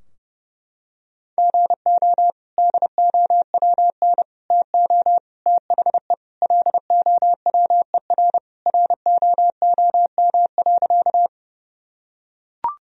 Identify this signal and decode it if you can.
Morse code, heard as a keyed tone, about 32 wpm, 700 Hz. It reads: GO DOWN TO THE LOWER ROOM.
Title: Sentences from Top 1000 Words - Review Part 4 - 30wpm